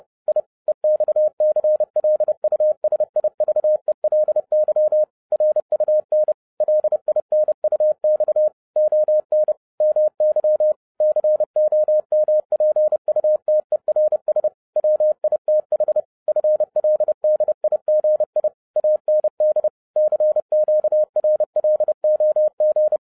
Morse code, heard as a keyed tone, about 30 wpm, 610 Hz. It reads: I EXCLUSIVELY RUN LINUX ON MY COMPUTERS WITH FLDIGI AND CQRLOG